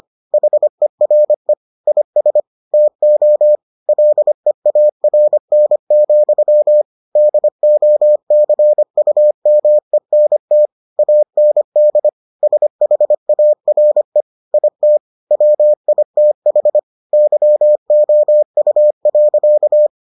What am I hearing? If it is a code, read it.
Morse code, heard as a keyed tone, about 25 wpm, 600 Hz. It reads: HERE IS TO LEARN, DOCUMENT AND SHARE IT WITH YOU.